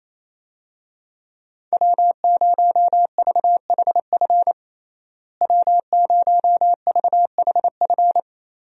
Morse code, heard as a keyed tone, about 28 wpm, 700 Hz. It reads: W0VHF W0VHF
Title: USA Call Signs - 5 Characters - Repeated 3x - 28wpm